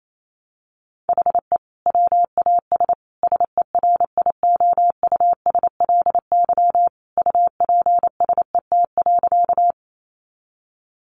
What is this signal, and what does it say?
Morse code, heard as a keyed tone, about 28 wpm, 700 Hz. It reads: HE WAS SERIOUSLY UPSET.
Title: Sentences from Top 1200 Words 28wpm